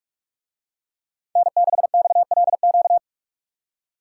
Morse code, heard as a keyed tone, about 48 wpm, 700 Hz. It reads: N6XLQ